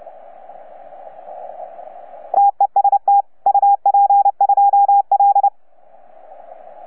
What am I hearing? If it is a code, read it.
Morse code, heard as a keyed tone, about 29 wpm, 810 Hz. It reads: TEST UP2L
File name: UP2L-20-CW